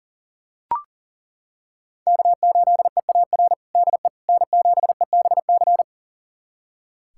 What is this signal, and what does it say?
Morse code, heard as a keyed tone, about 37 wpm, 700 Hz. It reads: K8EAR DE N7EBC
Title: USA Callsign DE Callsign - 5 Characters - 40wpm